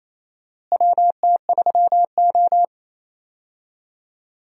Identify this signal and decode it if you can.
Morse code, heard as a keyed tone, about 28 wpm, 700 Hz. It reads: WT3O
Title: USA Call Signs - 4 Characters - Repeated 2x - 28wpm